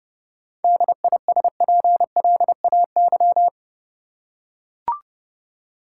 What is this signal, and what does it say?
Morse code, heard as a keyed tone, about 32 wpm, 700 Hz. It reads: DISPLAY
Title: Seven Letter Words - 30wpm